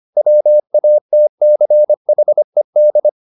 Morse code, heard as a keyed tone, about 25 wpm, 600 Hz. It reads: WATCHED